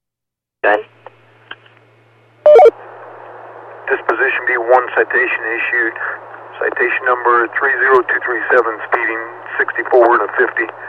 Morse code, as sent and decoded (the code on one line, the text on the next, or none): none